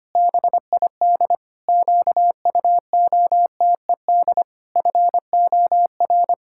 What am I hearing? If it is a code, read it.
Morse code, heard as a keyed tone, about 25 wpm, 700 Hz. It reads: BID QUOTED FOR
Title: Pangram Challenge v2 - 25wpm